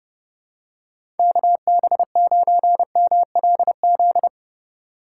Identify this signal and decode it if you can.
Morse code, heard as a keyed tone, about 30 wpm, 700 Hz. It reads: KB9MLZ
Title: USA Call Signs - 6 Characters - 3x Character Spacing - 30wpm